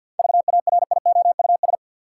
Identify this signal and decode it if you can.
Morse code, heard as a keyed tone, about 50 wpm, 700 Hz. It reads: VARIOUS
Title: Seven Letter Words - 50wpm